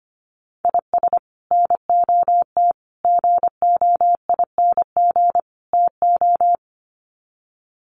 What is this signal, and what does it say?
Morse code, heard as a keyed tone, about 25 wpm, 700 Hz. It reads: IS NOT GOING TO
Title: Sets of 4 Words 25wpm